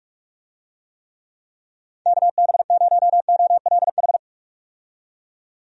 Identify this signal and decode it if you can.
Morse code, heard as a keyed tone, about 42 wpm, 700 Hz. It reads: KB0OLH
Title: USA Call Signs - 6 Characters - 45wpm